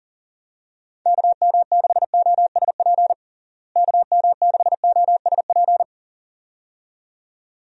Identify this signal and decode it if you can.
Morse code, heard as a keyed tone, about 37 wpm, 700 Hz. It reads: KM6OSP KM6OSP
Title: USA Call Signs - 6 Characters - Repeated 2x - 40wpm